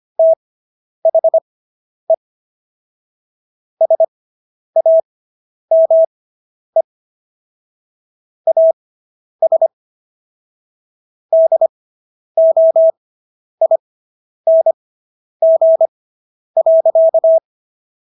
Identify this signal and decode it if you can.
Morse code, heard as a keyed tone, about 25 wpm, spaced as for 10 wpm, 650 Hz. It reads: THE SAME AS DOING.